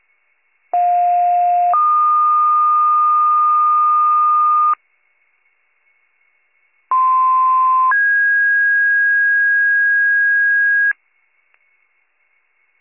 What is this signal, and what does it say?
Two-tone page, heard as a one-second tone, then a three-second tone, about 710 then 1220 Hz, about 1040 then 1650 Hz. It is heard from 0.7 to 10.9 s.